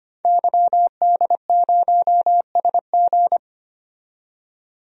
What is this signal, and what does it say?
Morse code, heard as a keyed tone, about 25 wpm, 700 Hz. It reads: YD0SG
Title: DXCC-100 Call Signs - 5 Characters - 25wpm